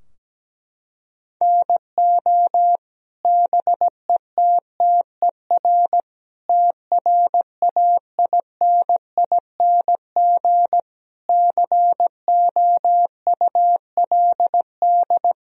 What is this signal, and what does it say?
Morse code, heard as a keyed tone, about 17 wpm, 700 Hz. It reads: NO BETTER TRAINING COULD
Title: Sentences from Top 1000 Words - Review Part 4 - 17wpm